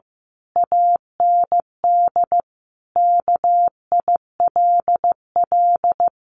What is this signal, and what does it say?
Morse code, heard as a keyed tone, about 15 wpm, 700 Hz. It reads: AND KILL
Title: A Sound Of Thunder 15wpm